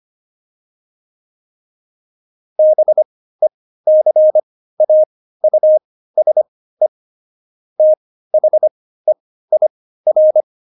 Morse code, heard as a keyed tone, about 25 wpm, spaced as for 15 wpm, 620 Hz. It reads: BECAUSE THEIR
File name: Jokes-25-char-15-eff